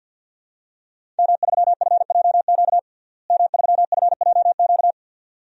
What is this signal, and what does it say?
Morse code, heard as a keyed tone, about 51 wpm, 700 Hz. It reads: M3FJQ M3FJQ
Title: DXCC-100 Call Signs - 5 Characters - Repeated 2x - 50wpm